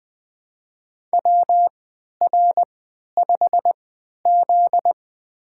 Morse code, heard as a keyed tone, about 20 wpm, spaced as for 11 wpm, 700 Hz. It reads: WR5Z
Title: USA Call Signs - 3x Character Spacing - 20wpm